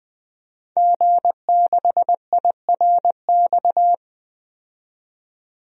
Morse code, heard as a keyed tone, about 20 wpm, 700 Hz. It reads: G6IRX
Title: DXCC-100 Call Signs - 5 Characters - 20wpm